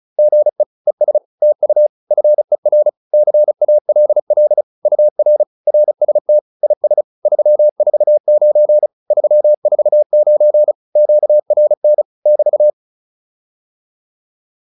Morse code, heard as a keyed tone, about 34 wpm, 600 Hz. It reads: GE ES TU FER CALL UR RST IS 349 349 QRN <BT>